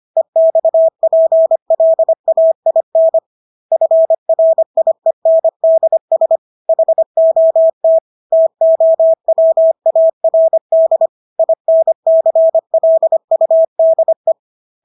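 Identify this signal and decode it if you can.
Morse code, heard as a keyed tone, about 25 wpm, 650 Hz. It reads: EXPLAIN FRIENDS HOT TOWARD INCLUDE
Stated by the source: Morse practice of English words at 25 wpm from CW med Gnister.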